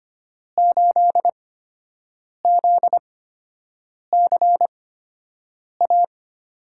Morse code, heard as a keyed tone, about 25 wpm, spaced as for 7 wpm, 700 Hz. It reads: 8ZCA
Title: USA Call Signs - 5 Characters - 8x Character Spacing - 25wpm